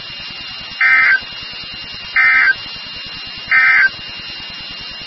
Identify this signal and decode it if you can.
Emergency Alert System (SAME) tones, heard as harsh buzzing bursts. It is heard from 0.7 to 4.0 s.
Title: NOAA Weather Radio WXK36